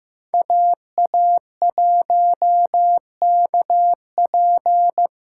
Morse code, heard as a keyed tone, about 15 wpm, 700 Hz. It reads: AA1KP
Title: USA Call Signs - 5 Characters - 4x Character Spacing - 15wpm